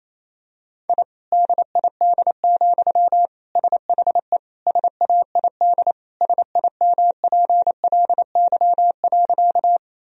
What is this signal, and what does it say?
Morse code, heard as a keyed tone, about 28 wpm, 700 Hz. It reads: I DID, SHE SAID SIMPLY.
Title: Sentences from Top 1000 Words - Review Part 4 - 28wpm